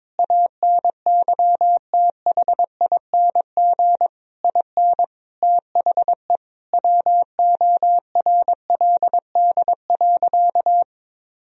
Morse code, heard as a keyed tone, about 22 wpm, 700 Hz. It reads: ANYTHING IN THE WORLD.
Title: Sentences from Top 1000 Words - Review Part 3 - 22wpm